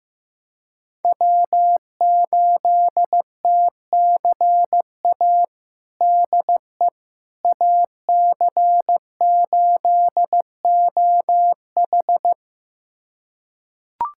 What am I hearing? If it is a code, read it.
Morse code, heard as a keyed tone, about 16 wpm, 700 Hz. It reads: W8TCA DE AC8OH
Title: USA Callsign DE Callsign - 5 Characters - 15wpm